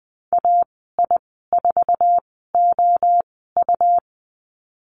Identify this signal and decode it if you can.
Morse code, heard as a keyed tone, about 20 wpm, spaced as for 15 wpm, 700 Hz. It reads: AI4OU
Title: 128 Lesson 21g - Number 6 - Call Signs 20wpm